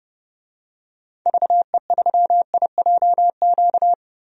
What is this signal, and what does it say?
Morse code, heard as a keyed tone, about 30 wpm, 700 Hz. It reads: VE3IJQ
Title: DXCC-100 Call Signs - 6 Characters - Repeated 2x - 30wpm